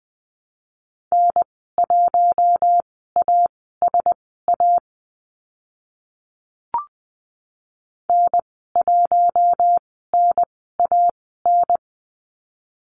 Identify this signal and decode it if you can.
Morse code, heard as a keyed tone, about 20 wpm, spaced as for 14 wpm, 700 Hz. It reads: N1ASA N1NAN
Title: 042 Lesson 8g - Number 1 - Call Signs 20wpm